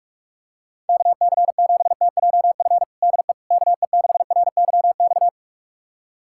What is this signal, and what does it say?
Morse code, heard as a keyed tone, about 42 wpm, 700 Hz. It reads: KC7TJF DE KE6RYX
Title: USA Callsign DE Callsign - 6 Characters - 45wpm